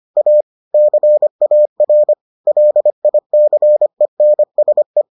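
Morse code, heard as a keyed tone, about 25 wpm, 600 Hz. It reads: A CAR LICENSE
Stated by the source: Morse edition of the Foundations of Amateur Radio episodes.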